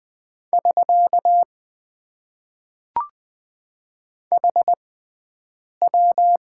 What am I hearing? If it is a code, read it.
Morse code, heard as a keyed tone, about 20 wpm, spaced as for 7 wpm, 700 Hz. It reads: <SK> HW
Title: Top 90 QSO Elements - 6x Character Spacing - 20wpm